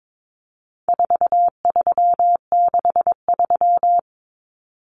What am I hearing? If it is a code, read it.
Morse code, heard as a keyed tone, about 22 wpm, 700 Hz. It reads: 4363
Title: Common Cut Numbers - 4-Digit 22wpm